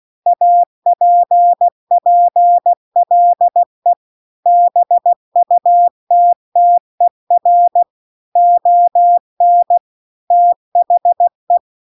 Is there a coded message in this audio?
Yes: Morse code from 0.3 to 11.6 s, heard as a keyed tone, about 16 wpm, 700 Hz. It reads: APPLE BUTTER ON THE